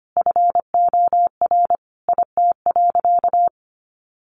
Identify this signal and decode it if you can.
Morse code, heard as a keyed tone, about 25 wpm, 700 Hz. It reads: FOR IT.
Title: Sentences from Top 1100 Words 25wpm